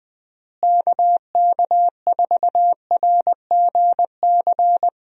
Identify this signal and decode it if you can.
Morse code, heard as a keyed tone, about 21 wpm, 700 Hz. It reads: KK4RGC
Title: USA Call Signs - 6 Characters - Repeated 3x - 20wpm